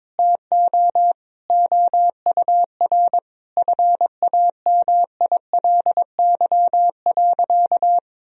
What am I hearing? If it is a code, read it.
Morse code, heard as a keyed tone, about 22 wpm, 700 Hz. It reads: TO OUR FAMILY.
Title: Sentences from Top 1000 Words - Review Part 4 - 22wpm